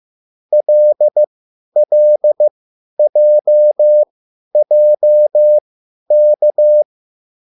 Morse code, heard as a keyed tone, about 15 wpm, spaced as for 10 wpm, 600 Hz. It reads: LLJJK